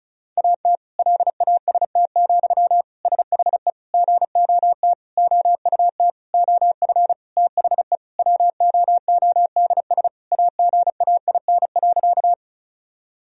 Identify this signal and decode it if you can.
Morse code, heard as a keyed tone, about 36 wpm, 700 Hz. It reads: AT LAST, SHE GOT OUT OF THE WOODS AGAIN.
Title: Sentences from Top 1000 Words - Review Part 2 - 35wpm